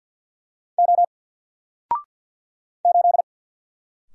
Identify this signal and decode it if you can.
Morse code, heard as a keyed tone, about 50 wpm, 700 Hz. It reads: O 8
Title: Single Letter-Number 50wpm